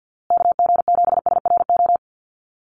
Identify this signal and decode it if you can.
Morse code, heard as a keyed tone, about 50 wpm, 700 Hz. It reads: KG7SRO
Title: USA Call Signs Repeated 3x 50wpm